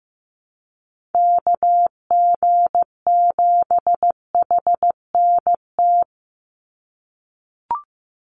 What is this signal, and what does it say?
Morse code, heard as a keyed tone, about 15 wpm, 700 Hz. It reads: KG7HNT
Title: USA Call Signs 15wpm